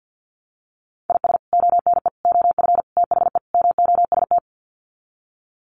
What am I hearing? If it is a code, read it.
Morse code, heard as a keyed tone, about 50 wpm, 700 Hz. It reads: IS ONE OF THE MOST
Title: Sets of 5 Words 50wpm